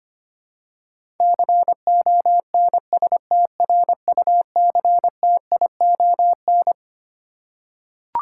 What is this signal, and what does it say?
Morse code, heard as a keyed tone, about 25 wpm, 700 Hz. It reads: CONSTRUCTION
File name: Twelve Letter Words - 25wpm